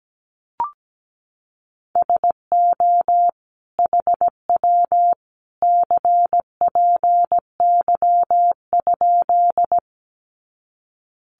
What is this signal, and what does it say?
Morse code, heard as a keyed tone, about 17 wpm, 700 Hz. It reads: SO HW CPY?
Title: Common QSO Phrases 17wpm